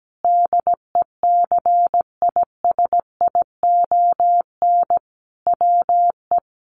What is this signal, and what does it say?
Morse code, heard as a keyed tone, about 17 wpm, 700 Hz. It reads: DECISION WE
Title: Sentences from Top 1000 Words 17wpm